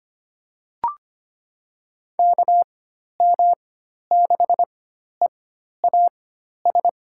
Morse code, heard as a keyed tone, about 25 wpm, spaced as for 12 wpm, 700 Hz. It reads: KM6EAS
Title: USA Call Signs - 6 Characters - 4x Character Spacing - 25wpm